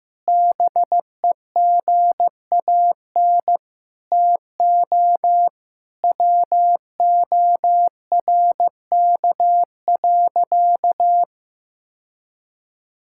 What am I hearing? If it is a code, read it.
Morse code, heard as a keyed tone, about 15 wpm, 700 Hz. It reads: BEGAN TO WORK.
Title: Sentences from Top 1000 Words - Review Part 6 - 15wpm